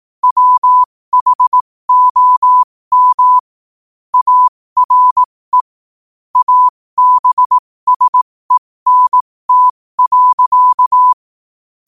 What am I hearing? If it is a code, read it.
Morse code, heard as a keyed tone, about 18 wpm, spaced as for 15 wpm, 1000 Hz. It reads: WHOM ARE ABSENT.
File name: qotdfiles / 15wpm